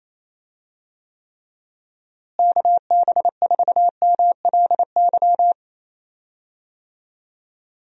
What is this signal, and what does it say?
Morse code, heard as a keyed tone, about 28 wpm, 700 Hz. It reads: KB4MLY
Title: USA Call Signs - 6 Characters - Repeated 2x - 28wpm